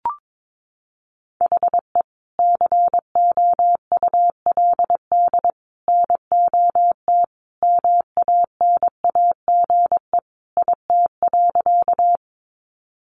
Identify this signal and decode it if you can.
Morse code, heard as a keyed tone, about 22 wpm, 700 Hz. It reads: HE COULD NOT MANAGE IT.
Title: Sentences from Top 1400 Words 22wpm